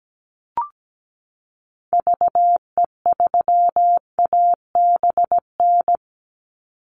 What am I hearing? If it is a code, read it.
Morse code, heard as a keyed tone, about 17 wpm, 700 Hz. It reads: VE3ABN
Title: Call Signs Repeated 2x - 17wpm